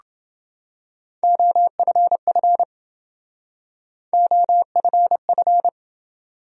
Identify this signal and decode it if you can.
Morse code, heard as a keyed tone, about 27 wpm, 700 Hz. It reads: OFF OFF